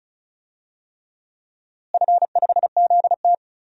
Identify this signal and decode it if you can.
Morse code, heard as a keyed tone, about 36 wpm, 700 Hz. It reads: F5ZT
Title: DXCC-100 Call Signs - 4 Characters - Repeated 3x - 35wpm